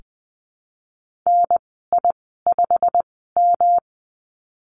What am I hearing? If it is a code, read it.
Morse code, heard as a keyed tone, about 20 wpm, spaced as for 15 wpm, 700 Hz. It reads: NI5M
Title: 128 Lesson 21g - Number 6 - Call Signs 20wpm